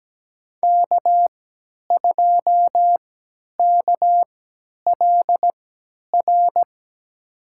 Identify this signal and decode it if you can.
Morse code, heard as a keyed tone, about 17 wpm, spaced as for 10 wpm, 700 Hz. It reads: K2KLR
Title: USA Call Signs - 5 Characters - 3x Character Spacing - 17wpm